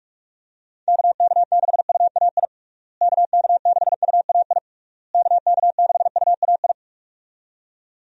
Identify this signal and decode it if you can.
Morse code, heard as a keyed tone, about 46 wpm, 700 Hz. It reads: KK6UAI KK6UAI KK6UAI
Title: USA Call Signs - 6 Characters - Repeated 3x - 45wpm